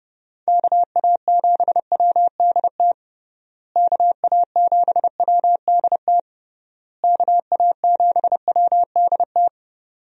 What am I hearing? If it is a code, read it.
Morse code, heard as a keyed tone, about 30 wpm, 700 Hz. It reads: KA7WDT KA7WDT KA7WDT
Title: USA Call Signs - 6 Characters - Repeated 3x - 30wpm